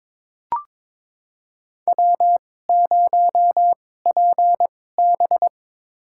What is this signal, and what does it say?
Morse code, heard as a keyed tone, about 21 wpm, spaced as for 16 wpm, 700 Hz. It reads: W0PB
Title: USA Call Signs - 2x Character Spacing - 22wpm